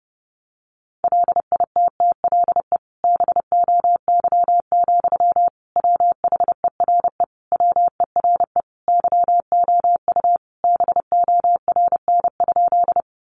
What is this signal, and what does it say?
Morse code, heard as a keyed tone, about 30 wpm, 700 Hz. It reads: LITTLE BOY, WHERE WERE YOU BORN?
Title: Sentences from Top 1000 Words 30wpm